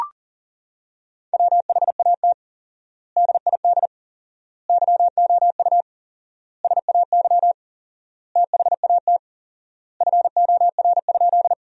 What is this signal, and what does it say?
Morse code, heard as a keyed tone, about 40 wpm, 700 Hz. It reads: WHAT DID YOU SAY THAT FOR?
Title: 4x Word-Spacing - Sentences from Top 200 Words - 40wpm